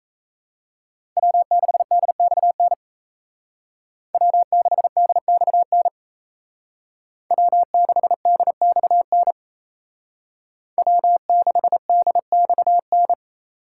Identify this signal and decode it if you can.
Morse code, heard as a keyed tone, about 36 wpm, 700 Hz. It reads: W6DXN W6DXN W6DXN W6DXN